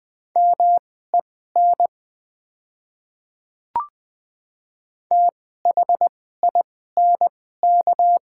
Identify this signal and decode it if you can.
Morse code, heard as a keyed tone, about 20 wpm, spaced as for 14 wpm, 690 Hz. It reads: MEN THINK
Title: 2x Character Spacing - Top 200 Words - 20wpm